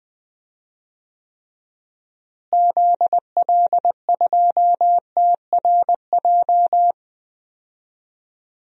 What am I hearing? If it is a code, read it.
Morse code, heard as a keyed tone, about 20 wpm, 700 Hz. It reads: ZL2TRJ
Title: DXCC-100 Call Signs - 6 Characters - Repeated 2x - 20wpm